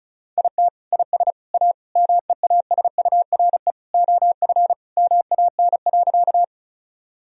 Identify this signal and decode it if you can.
Morse code, heard as a keyed tone, about 36 wpm, 700 Hz. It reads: IT IS A MEASURE OF MAN.
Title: Sentences from Top 1000 Words - Review Part 2 - 35wpm